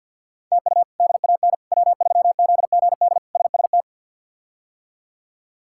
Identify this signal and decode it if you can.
Morse code, heard as a keyed tone, about 45 wpm, 700 Hz. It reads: TU DAN W3ZGD SST
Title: K1USN SST QSO Practice - 50wpm